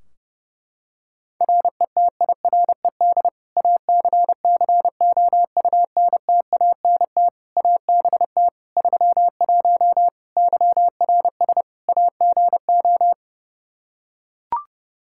Morse code, heard as a keyed tone, about 30 wpm, 700 Hz. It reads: RETIRED ACCOUNTANT ABT 31 YRS AGO
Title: K1USN SST QSO Practice 30wpm